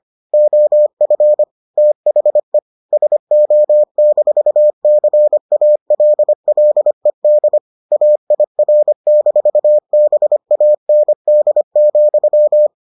Morse code, heard as a keyed tone, about 25 wpm, 600 Hz. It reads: OF THE SO-CALLED AIR-BAND,